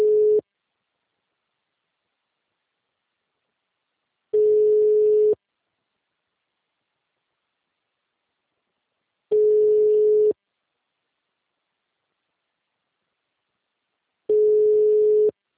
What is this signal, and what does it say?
Ringback tone (Europe, 425 Hz).
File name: out- -2006- - - .367